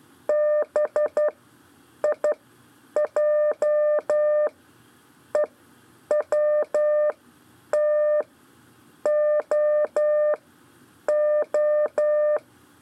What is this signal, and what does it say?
Morse code, heard as a keyed tone, about 12 wpm, spaced as for 8 wpm, 600 Hz. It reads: BIJEWTOO
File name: 4 LEZIONE CW 1